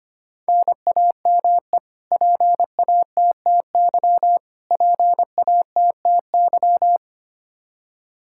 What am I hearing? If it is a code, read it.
Morse code, heard as a keyed tone, about 25 wpm, 700 Hz. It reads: NAME PATTY PATTY
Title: Practice QSO - Exchange 1 - Beginning - 25wpm